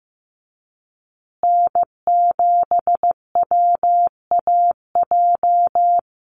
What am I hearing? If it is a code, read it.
Morse code, heard as a keyed tone, about 15 wpm, 700 Hz. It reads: N7WAJ
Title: USA Call Signs Repeated 2x 15wpm